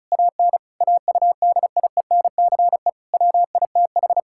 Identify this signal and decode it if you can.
Morse code, heard as a keyed tone, about 34 wpm, 700 Hz. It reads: AN AUDIENCE WITH
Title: Sentences from Top 1000 Words - Review Part 5 - 35wpm